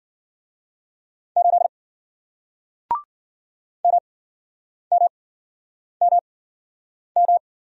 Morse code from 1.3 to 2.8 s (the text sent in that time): 8